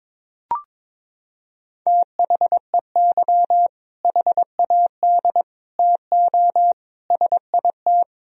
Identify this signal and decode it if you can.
Morse code, heard as a keyed tone, about 22 wpm, 700 Hz. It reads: THEY HAD TO SIT
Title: Sentences from Top 1000 Words - Review Part 1 - 22wpm